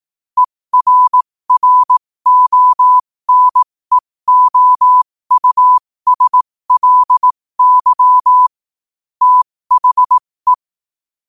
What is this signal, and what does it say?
Morse code, heard as a keyed tone, about 18 wpm, spaced as for 15 wpm, 1000 Hz. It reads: ERRONEOUSLY THE
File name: qotdfiles / 15wpm